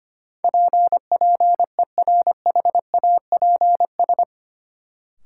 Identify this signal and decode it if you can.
Morse code, heard as a keyed tone, about 25 wpm, 700 Hz. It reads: PPERHAPS
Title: Sentences from Top 1100 Words - New Word Drill - 25wpm